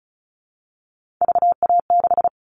Morse code, heard as a keyed tone, about 34 wpm, 700 Hz. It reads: VA6
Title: All Cut Numbers - 3-Digit 35wpm